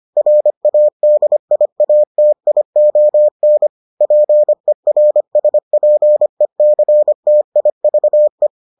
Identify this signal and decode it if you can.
Morse code, heard as a keyed tone, about 25 wpm, 600 Hz. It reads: RADIATION PERSPECTIVE